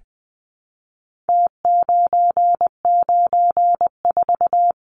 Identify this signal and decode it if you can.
Morse code, heard as a keyed tone, about 20 wpm, 700 Hz. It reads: T994
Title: Numbers - Contest Style - 4-Digit 20wpm